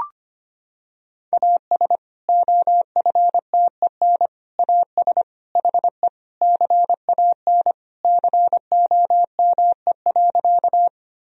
Morse code, heard as a keyed tone, about 25 wpm, 700 Hz. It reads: AS OFTEN AS HE CAN COME.
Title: Sentences from Top 1000 Words - Review Part 1 - 25wpm